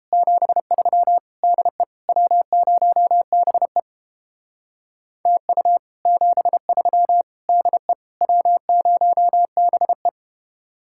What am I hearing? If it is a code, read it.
Morse code, heard as a keyed tone, about 31 wpm, 700 Hz. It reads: 73 DE W0BE TU 73 DE W0BE